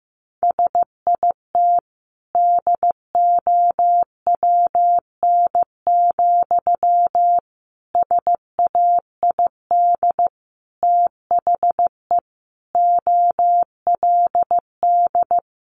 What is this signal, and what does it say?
Morse code, heard as a keyed tone, about 15 wpm, 700 Hz. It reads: SIT DOWN, SAID THE OLD
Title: Sentences from Top 500 Words 15wpm